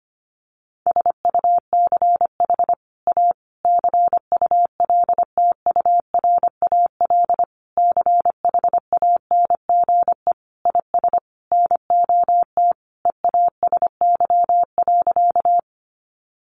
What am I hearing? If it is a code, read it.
Morse code, heard as a keyed tone, about 25 wpm, 700 Hz. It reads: SUCH A CULTURAL CHANGE IS NOT EASY.
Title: Sentences from Top 1500 Words 25wpm